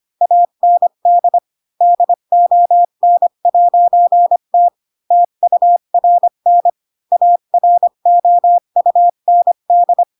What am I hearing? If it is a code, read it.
Morse code, heard as a keyed tone, about 25 wpm, 700 Hz. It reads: AND DON'T TURN AROUND
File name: Pinocchio / Pinocchio ch4 20 25 ik1hge